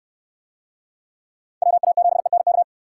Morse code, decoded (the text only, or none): WA7ENK